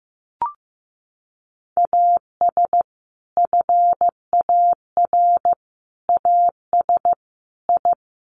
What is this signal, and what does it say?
Morse code, heard as a keyed tone, about 15 wpm, 700 Hz. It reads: AS FAR AS I
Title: Sets of 5 Words 15wpm